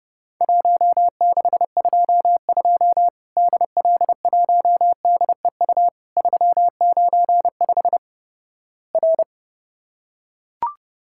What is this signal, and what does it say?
Morse code, heard as a keyed tone, about 30 wpm, 700 Hz. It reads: 1622 DL1DEU 395 R
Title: Worked All Europe DX Contest - QTC Exchange - 30wpm